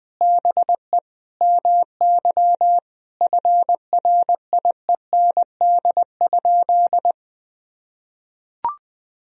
Morse code, heard as a keyed tone, about 20 wpm, 700 Hz. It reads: BE MY FRIEND?
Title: Sentences from Top 1000 Words - Review Part 3 - 20wpm